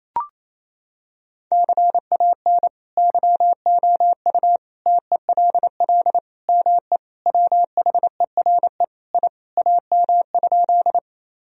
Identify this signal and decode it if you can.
Morse code, heard as a keyed tone, about 28 wpm, 700 Hz. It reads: CAN YOU TELL ME WHERE I AM?
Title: Sentences from Top 1000 Words - Review Part 2 - 28wpm